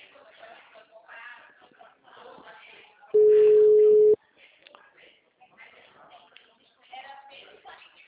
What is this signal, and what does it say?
Ringback tone (Europe, 425 Hz), with speech.
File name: out- -2002- - - .31